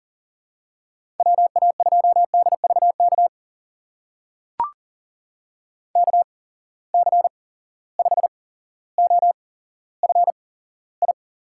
Morse code, heard as a keyed tone, about 40 wpm, spaced as for 17 wpm, 700 Hz. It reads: WA2DVK KC5OFI
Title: USA Call Signs - 6 Characters - 8x Character Spacing - 40wpm